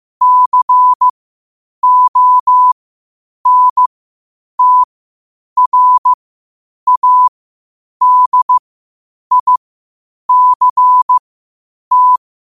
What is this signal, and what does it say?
Morse code, heard as a keyed tone, about 15 wpm, spaced as for 8 wpm, 1000 Hz. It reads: CONTRADICT
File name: qotdfiles / 8wpm